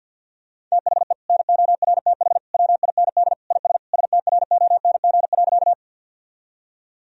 Morse code, heard as a keyed tone, about 52 wpm, 700 Hz. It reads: THE NORTH WIND IS STRONG.
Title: Sentences from Top 1000 Words - Review Part 2 - 50wpm